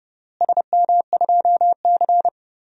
Morse code, heard as a keyed tone, about 30 wpm, 700 Hz. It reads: SM2C
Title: DXCC-100 Call Signs - 4 Characters - Repeated 3x - 30wpm